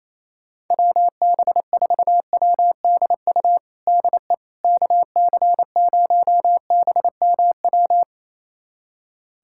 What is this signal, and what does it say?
Morse code, heard as a keyed tone, about 28 wpm, 700 Hz. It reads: WB4WDU DE KC0BMW
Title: USA Callsign DE Callsign - 6 Characters - 28wpm